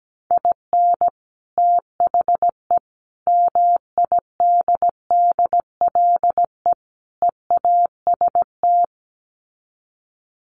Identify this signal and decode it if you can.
Morse code, heard as a keyed tone, about 17 wpm, 700 Hz. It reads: IN THE MIDDLE EAST
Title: Sets of 4 Words 17wpm